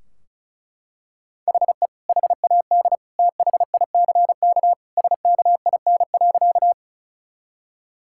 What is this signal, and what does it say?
Morse code, heard as a keyed tone, about 36 wpm, 700 Hz. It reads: HE HAD THICK SKIN.
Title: Sentences from Top 1600 Words 35wpm